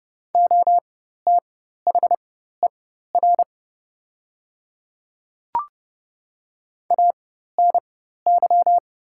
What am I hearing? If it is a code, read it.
Morse code, heard as a keyed tone, about 30 wpm, spaced as for 14 wpm, 700 Hz. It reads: OTHER ANY
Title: 4x Character Spacing - Top 300 Words - 30wpm